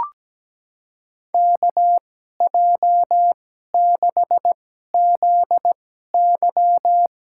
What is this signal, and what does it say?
Morse code, heard as a keyed tone, about 17 wpm, spaced as for 12 wpm, 700 Hz. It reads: KJ6ZY
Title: USA Call Signs - 5 Characters - 2x Character Spacing - 17wpm